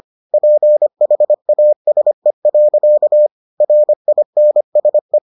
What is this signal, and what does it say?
Morse code, heard as a keyed tone, about 25 wpm, 600 Hz. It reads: PHASE. RINSE